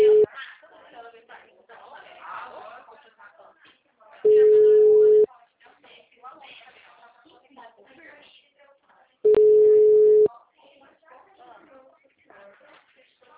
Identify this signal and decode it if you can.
Ringback tone (Europe, 425 Hz), with speech.
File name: out- -2027